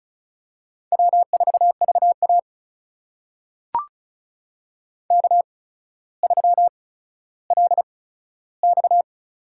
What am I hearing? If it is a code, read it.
Morse code, heard as a keyed tone, about 36 wpm, spaced as for 15 wpm, 700 Hz. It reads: W4VA K3LX
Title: USA Call Signs - 4 Characters - 8x Character Spacing - 35wpm